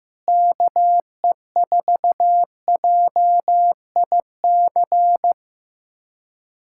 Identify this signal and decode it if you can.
Morse code, heard as a keyed tone, about 15 wpm, 700 Hz. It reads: KE4JIC
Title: USA Callsign DE Callsign - 6 Characters - 15wpm